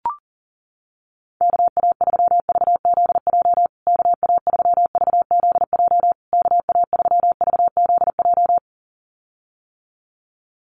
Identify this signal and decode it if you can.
Morse code, heard as a keyed tone, about 40 wpm, 700 Hz. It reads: KA3VZJ KA3VZJ KA3VZJ
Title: USA Call Signs Repeated 3x 40wpm